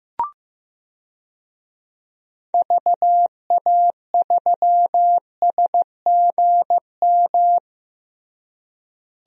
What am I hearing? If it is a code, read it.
Morse code, heard as a keyed tone, about 15 wpm, 700 Hz. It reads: VA3SGM
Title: DXCC-100 Call Signs - 6 Characters - Repeated 2x - 15wpm